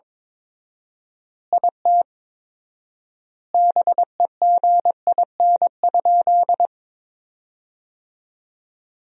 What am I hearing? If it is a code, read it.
Morse code, heard as a keyed tone, about 22 wpm, 700 Hz. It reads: IT BEGIN?
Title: 4x Word-Spacing - Sentences from Top 300 Words - 22wpm